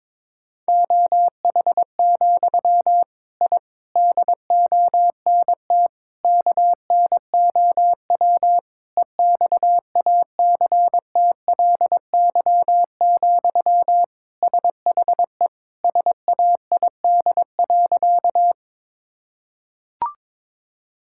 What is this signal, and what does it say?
Morse code, heard as a keyed tone, about 22 wpm, 700 Hz. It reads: OH, I DONT KNOW EXACTLY, SHE SAID.
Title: Sentences from Top 1000 Words - Review Part 6 - 22wpm